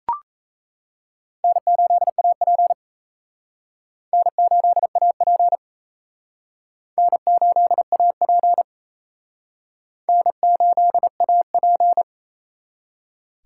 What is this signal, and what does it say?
Morse code, heard as a keyed tone, about 34 wpm, 700 Hz. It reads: N8AP N8AP N8AP N8AP
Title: Speed Racing - USA 1x2 Call Signs 28wpm